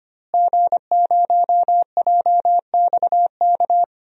Morse code, heard as a keyed tone, about 25 wpm, 700 Hz. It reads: G0JXK
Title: DXCC-100 Call Signs - 5 Characters - Repeated 2x - 25wpm